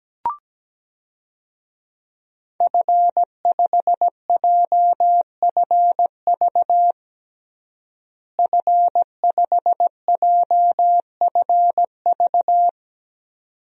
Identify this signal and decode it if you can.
Morse code, heard as a keyed tone, about 17 wpm, 700 Hz. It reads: F5JFV F5JFV
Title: DXCC-100 Call Signs - 5 Characters - Repeated 2x - 17wpm